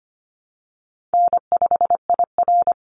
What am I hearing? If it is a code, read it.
Morse code, heard as a keyed tone, about 25 wpm, 700 Hz. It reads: N5IR
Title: Easy Call Signs - 25wpm